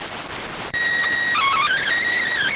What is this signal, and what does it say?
SSTV (slow-scan television), PD 290.